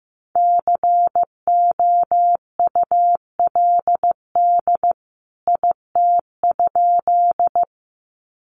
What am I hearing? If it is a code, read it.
Morse code, heard as a keyed tone, about 15 wpm, 700 Hz. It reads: COULD IT?
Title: A Sound Of Thunder 15wpm